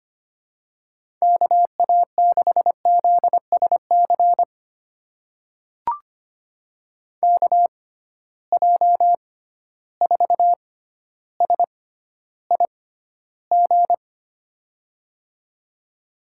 Morse code, heard as a keyed tone, about 25 wpm, spaced as for 13 wpm, 700 Hz. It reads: KA6ZSC KJ4SIG
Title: USA Call Signs - 6 Characters - 6x Character Spacing - 25wpm